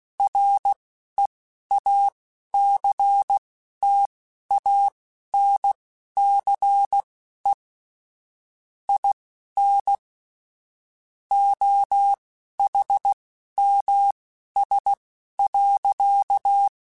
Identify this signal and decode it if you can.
Morse code, heard as a keyed tone, about 16 wpm, spaced as for 11 wpm, 800 Hz. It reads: REACTANCE IN OHMS.